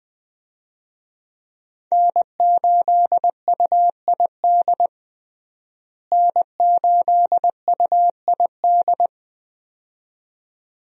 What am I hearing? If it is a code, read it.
Morse code, heard as a keyed tone, about 20 wpm, 700 Hz. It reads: N8UID N8UID
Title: USA Call Signs - 5 Characters - Repeated 2x - 20wpm